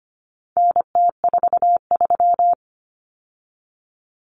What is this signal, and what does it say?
Morse code, heard as a keyed tone, about 25 wpm, 700 Hz. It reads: NT43
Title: Common Cut Numbers - 4-Digit 25wpm